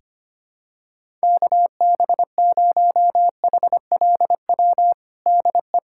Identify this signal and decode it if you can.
Morse code, heard as a keyed tone, about 25 wpm, 700 Hz. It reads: KB0HLW DE
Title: USA Callsign DE Callsign - 6 Characters - 25wpm